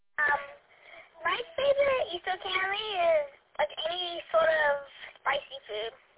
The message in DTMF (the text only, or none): none